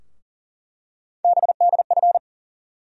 Morse code, heard as a keyed tone, about 40 wpm, 700 Hz. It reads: BDF
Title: USA Call Sign Postfixes - 3 Letters 40wpm